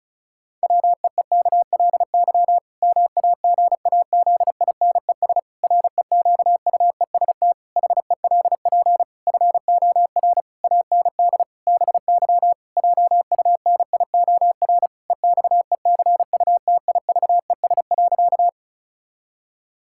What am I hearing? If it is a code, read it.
Morse code, heard as a keyed tone, about 36 wpm, 700 Hz. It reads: WEEKLY MAGAZINES REQUEST HELP FOR AND BY JUNIOR EXECUTIVES.
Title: Pangram Challenge v2 - 35wpm